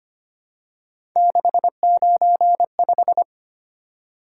Morse code, heard as a keyed tone, about 25 wpm, 700 Hz. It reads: 695
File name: Three Letter-Number Combinations - 25wpm